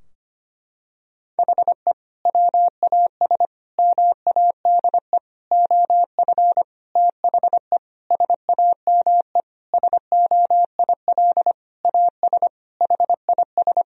Morse code, heard as a keyed tone, about 25 wpm, 700 Hz. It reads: HE WAS MADE OF THE SAME SOIL AS HIS
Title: Sentences from Top 1000 Words - Review Part 4 - 25wpm